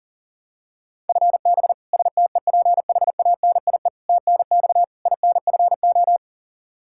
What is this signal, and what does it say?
Morse code, heard as a keyed tone, about 37 wpm, 700 Hz. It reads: FB STEPHANIE TNX INFO
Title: Practice QSO - Exchange 3 - More Info - 40wpm